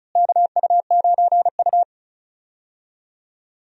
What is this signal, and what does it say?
Morse code, heard as a keyed tone, about 36 wpm, 700 Hz. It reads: KU9U
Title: USA Call Signs - 4 Characters - Repeated 3x - 35wpm